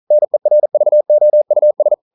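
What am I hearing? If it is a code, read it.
Morse code, heard as a keyed tone, about 42 wpm, 600 Hz. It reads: NERVOUS
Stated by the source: Morse code practice at 41 wpm, from CW med Gnister.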